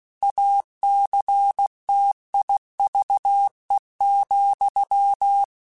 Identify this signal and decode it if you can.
Morse code, heard as a keyed tone, about 16 wpm, 800 Hz. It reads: ACTIVE,
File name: Technical Subjects / Technical Subjects-mp3s / AND Gate @15